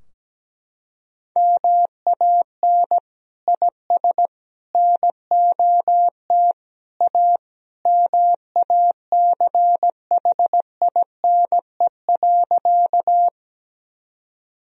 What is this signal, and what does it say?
Morse code, heard as a keyed tone, about 17 wpm, 700 Hz. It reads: MAN IS NOT A MACHINE.
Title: Sentences from Top 1000 Words - Review Part 2 - 17wpm